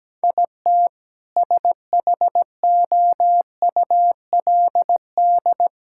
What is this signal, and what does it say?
Morse code, heard as a keyed tone, about 17 wpm, 700 Hz. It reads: IT SHOULD
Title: Sentences from Top 1600 Words 17wpm